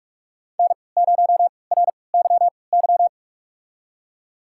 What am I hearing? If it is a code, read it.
Morse code, heard as a keyed tone, about 46 wpm, spaced as for 26 wpm, 700 Hz. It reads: N0RYY
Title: USA Call Signs - 5 Characters - 3x Character Spacing - 45wpm